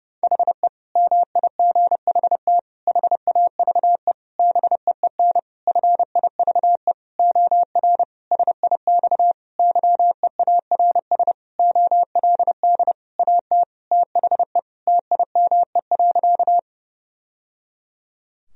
Morse code, heard as a keyed tone, about 30 wpm, 700 Hz. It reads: HE MIGHT HAVE BEEN FIVE OR SIX YEARS OLD AT THE TIME.
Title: Sentences from Top 1000 Words - Review Part 2 - 30wpm